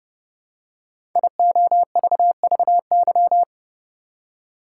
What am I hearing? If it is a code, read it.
Morse code, heard as a keyed tone, about 28 wpm, 700 Hz. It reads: IOVVY
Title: Code Groups - 5 Letter:Numbers - 30wpm